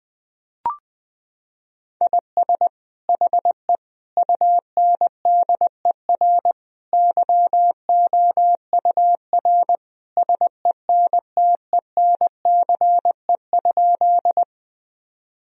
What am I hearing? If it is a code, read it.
Morse code, heard as a keyed tone, about 21 wpm, 700 Hz. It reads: IS HE UNDER YOUR SENTENCE?
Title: Sentences from Top 1000 Words - Review Part 1 - 20wpm